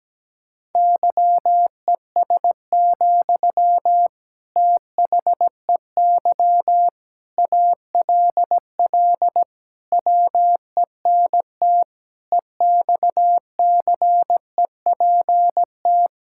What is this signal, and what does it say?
Morse code, heard as a keyed tone, about 17 wpm, 700 Hz. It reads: YES, THEY ALL WENT EXCEPT
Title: Sentences from Top 1600 Words 17wpm